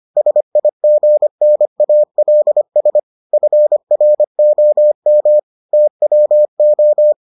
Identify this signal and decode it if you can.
Morse code, heard as a keyed tone, about 25 wpm, 600 Hz. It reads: SIGNALS FROM TWO